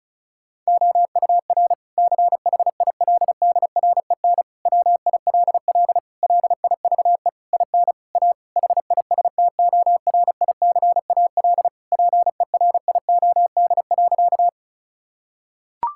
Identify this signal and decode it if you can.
Morse code, heard as a keyed tone, about 34 wpm, 700 Hz. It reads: OUR CHILDREN WILL LIVE IN A HISTORICAL PERIOD.
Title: Sentences from Top 1000 Words - Review Part 5 - 35wpm